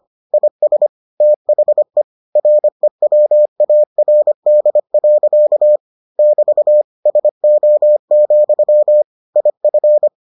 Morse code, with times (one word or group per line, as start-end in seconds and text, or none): 0.3-0.9 IS
1.2-2.0 THE
2.4-5.8 REWARD.
6.2-6.8 <BT>
7.1-9.0 SO,
9.4-10.1 IF